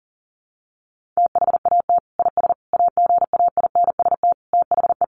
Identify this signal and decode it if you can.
Morse code, heard as a keyed tone, about 40 wpm, 700 Hz. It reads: THAT IS AGAINST THE
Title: Sentences from Top 1500 Words 40wpm